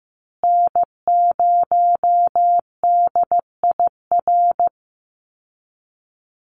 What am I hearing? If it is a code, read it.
Morse code, heard as a keyed tone, about 15 wpm, 700 Hz. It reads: N0DIR
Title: Call Signs Repeated 2x - 15wpm